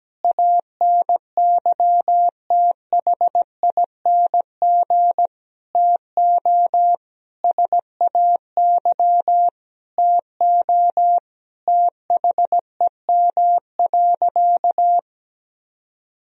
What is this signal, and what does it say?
Morse code, heard as a keyed tone, about 17 wpm, 700 Hz. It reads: ANYTHING TO SAY TO THEM.
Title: Sentences from Top 1000 Words - Review Part 1 - 17wpm